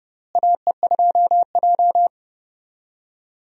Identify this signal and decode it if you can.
Morse code, heard as a keyed tone, about 32 wpm, 700 Hz. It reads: AE2J
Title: USA Call Signs - 4 Characters - Repeated 3x - 30wpm